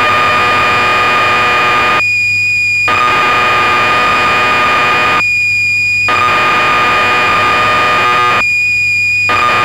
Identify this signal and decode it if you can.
Cassette data (Kansas City type).
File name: magazines / Acorn User / 1984-12 BBC Programs